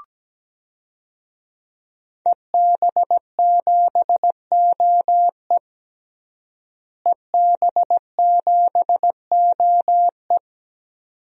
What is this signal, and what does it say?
Morse code, heard as a keyed tone, about 17 wpm, 700 Hz. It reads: EB7OE EB7OE
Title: DXCC-100 Call Signs - 5 Characters - Repeated 2x - 17wpm